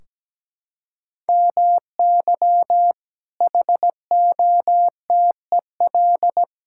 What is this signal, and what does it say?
Morse code, heard as a keyed tone, about 17 wpm, 700 Hz. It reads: MY HOTEL
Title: Sentences from Top 1600 Words 17wpm